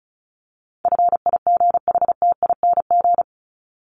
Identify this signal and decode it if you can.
Morse code, heard as a keyed tone, about 34 wpm, 700 Hz. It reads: FIGHTING
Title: ING Suffix - Top 100 Words with Base Words 35wpm